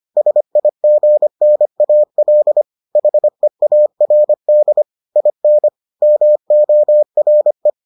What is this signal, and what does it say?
Morse code, heard as a keyed tone, about 25 wpm, 600 Hz. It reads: SIGNAL HEARD IN MORE